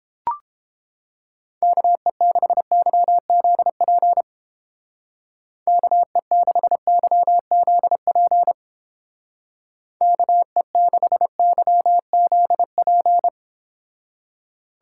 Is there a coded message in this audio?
Yes: Morse code from 1.6 to 13.3 s, heard as a keyed tone, about 30 wpm, 700 Hz. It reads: KE6YZP KE6YZP KE6YZP